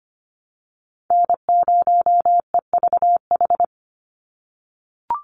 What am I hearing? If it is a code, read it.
Morse code, heard as a keyed tone, about 25 wpm, 700 Hz. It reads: N0EVH
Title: NA SOTA Chaser Callsigns 25wpm